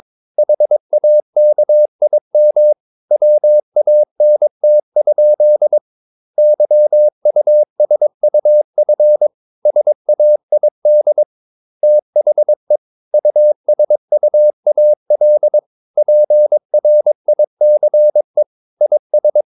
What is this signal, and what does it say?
Morse code, heard as a keyed tone, about 22 wpm, 600 Hz. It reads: HAKIM WANT? YUSUF SAID THE USUAL PRICE IS
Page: Romanen Never af Ken Follet på engelsk.